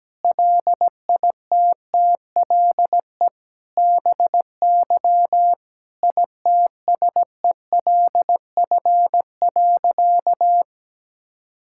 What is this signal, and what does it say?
Morse code, heard as a keyed tone, about 17 wpm, 700 Hz. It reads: LITTLE BY ITSELF.
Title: Sentences from Top 1000 Words - Review Part 5 - 17wpm